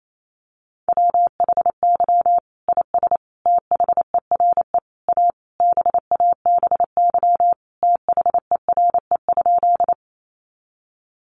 Morse code, heard as a keyed tone, about 28 wpm, 700 Hz. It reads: WHY IS THERE A BABY THERE?
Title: Sentences from Top 1300 Words 28wpm